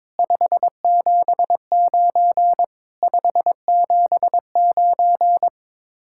Morse code, heard as a keyed tone, about 22 wpm, 700 Hz. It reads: 579 579
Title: Speed-Racing Common QSO Phrases - 22wpm